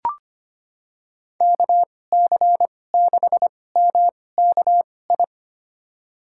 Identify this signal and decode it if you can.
Morse code, heard as a keyed tone, about 25 wpm, spaced as for 18 wpm, 700 Hz. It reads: KC6MKI
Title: USA Call Signs - 6 Characters - 2x Character Spacing - 25wpm